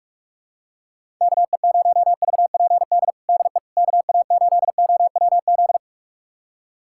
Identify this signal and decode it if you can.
Morse code, heard as a keyed tone, about 46 wpm, 700 Hz. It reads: KE0VPD DE KA8OWZ
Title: USA Callsign DE Callsign - 6 Characters - 45wpm